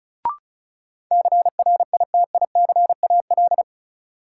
Morse code, heard as a keyed tone, about 36 wpm, 700 Hz. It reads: CRITICAL
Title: Sentences from Top 1100 Words - New Word Drill - 35wpm